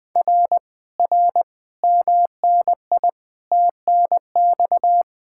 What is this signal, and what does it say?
Morse code, heard as a keyed tone, about 20 wpm, 700 Hz. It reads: R R MNI TNX
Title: Speed-Racing Common QSO Phrases - 17wpm